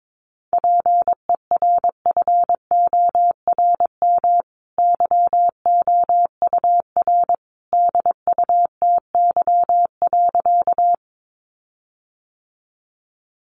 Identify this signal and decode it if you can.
Morse code, heard as a keyed tone, about 22 wpm, 700 Hz. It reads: PERFORM YOUR DUTY.
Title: Sentences from Top 1500 Words 22wpm